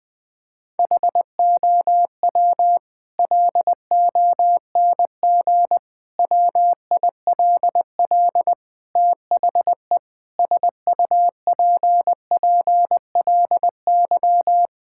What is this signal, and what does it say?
Morse code, heard as a keyed tone, about 20 wpm, 700 Hz. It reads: HOW LONG WILL THE SUPPLY
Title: Sentences from Top 1000 Words - Review Part 5 - 20wpm